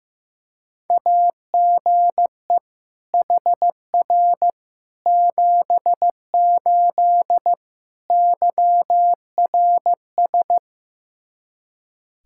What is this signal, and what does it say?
Morse code, heard as a keyed tone, about 15 wpm, 700 Hz. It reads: AGE HR 78 YRS
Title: Practice QSO - Exchange 3 - More Info - 15wpm